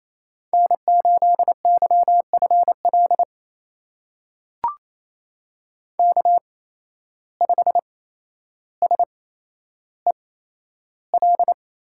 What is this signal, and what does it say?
Morse code, heard as a keyed tone, about 28 wpm, spaced as for 12 wpm, 700 Hz. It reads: N8YFL K5SEL
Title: USA Call Signs - 5 Characters - 8x Character Spacing - 28wpm